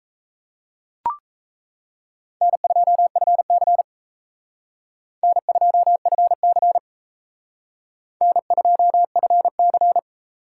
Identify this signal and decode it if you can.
Morse code, heard as a keyed tone, about 34 wpm, 700 Hz. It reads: N2FC N2FC N2FC
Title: Speed Racing - USA Call Signs 28wpm